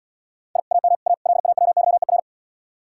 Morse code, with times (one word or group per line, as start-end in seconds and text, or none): 0.6-0.6 I
0.7-0.9 AM
1.1-1.1 A
1.3-2.2 LAWYER